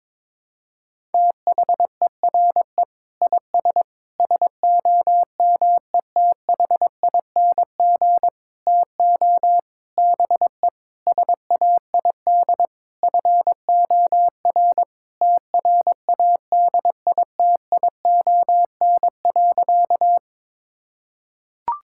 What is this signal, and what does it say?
Morse code, heard as a keyed tone, about 22 wpm, 700 Hz. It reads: THERE IS SOMETHING TO BE SAID FOR TRADITION.
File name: Sentences with 7-Word Phrases - 22wpm